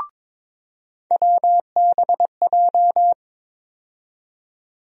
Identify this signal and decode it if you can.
Morse code, heard as a keyed tone, about 22 wpm, 700 Hz. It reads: WBJ
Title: USA Call Sign Postfixes - 3 Letters 22wpm